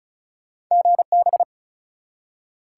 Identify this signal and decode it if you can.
Morse code, heard as a keyed tone, about 36 wpm, 700 Hz. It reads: GB
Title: Top 180 QSO Elements 35wpm